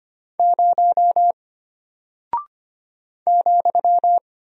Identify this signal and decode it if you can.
Morse code, heard as a keyed tone, about 25 wpm, 700 Hz. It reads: 0 ,
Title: Single Character (Letter:Number:Punctuation:Prosign) Alt 25wpm